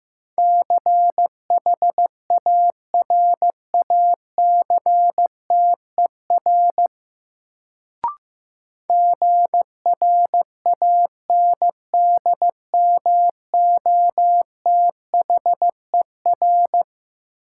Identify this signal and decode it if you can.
Morse code, heard as a keyed tone, about 15 wpm, 700 Hz. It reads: CHARACTER GRANDMOTHER
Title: Sentences from Top 1200 Words - New Word Drill Alt - 15wpm